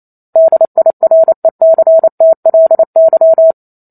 Morse code, heard as a keyed tone, about 28 wpm, 650 Hz. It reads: DIRECTLY
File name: Chapter 2 - 28 WPM